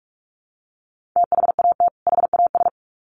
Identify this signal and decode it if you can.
Morse code, heard as a keyed tone, about 44 wpm, 700 Hz. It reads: THAT HAS
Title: Sets of 2 Words 45wpm